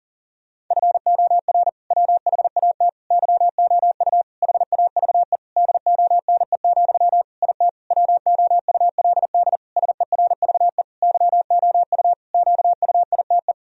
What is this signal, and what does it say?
Morse code, heard as a keyed tone, about 37 wpm, 700 Hz. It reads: FOR WHAT YOU HAVE DONE, IT WOULD SERVE YOU QUITE
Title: Sentences from Top 1000 Words - Review Part 2 - 40wpm